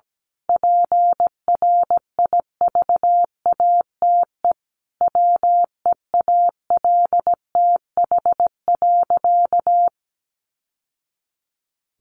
Morse code, heard as a keyed tone, about 17 wpm, 700 Hz. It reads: PRIVATE WEALTH.
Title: Sentences from Top 1200 Words 17wpm